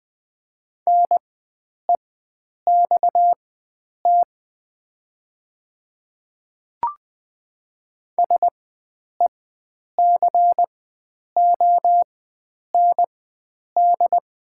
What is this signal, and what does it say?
Morse code, heard as a keyed tone, about 20 wpm, spaced as for 9 wpm, 700 Hz. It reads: NEXT SECOND
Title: 4x Character Spacing - Top 300 Words - 20wpm